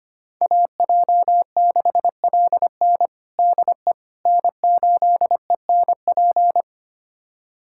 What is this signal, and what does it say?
Morse code, heard as a keyed tone, about 25 wpm, 700 Hz. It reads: AJ6LN DE N8ENP
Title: USA Callsign DE Callsign - 5 Characters - 25wpm